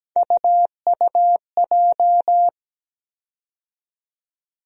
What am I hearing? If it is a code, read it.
Morse code, heard as a keyed tone, about 17 wpm, 700 Hz. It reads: UUJ
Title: USA Call Sign Postfixes - 3 Letters 17wpm